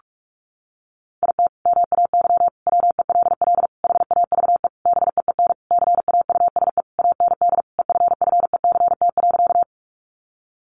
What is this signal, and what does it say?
Morse code, heard as a keyed tone, about 46 wpm, 700 Hz. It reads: IT MAY WELL HAVE BEEN CAUSE AND EFFECT.
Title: Sentences from Top 1000 Words 45wpm